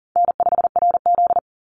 Morse code, heard as a keyed tone, about 40 wpm, 700 Hz. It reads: N5RZ
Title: Easy Call Signs Repeated 3x - 40wpm